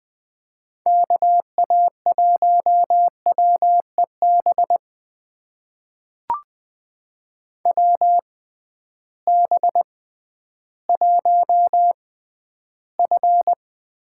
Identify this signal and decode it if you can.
Morse code, heard as a keyed tone, about 20 wpm, spaced as for 12 wpm, 700 Hz. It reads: KA1WEB WB1F
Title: USA Call Signs - 6 Characters - 6x Character Spacing - 20wpm